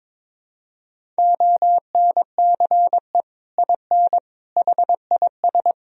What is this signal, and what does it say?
Morse code, heard as a keyed tone, about 22 wpm, 700 Hz. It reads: ONCE IN HIS
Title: Sentences from Top 1000 Words - Review Part 2 - 22wpm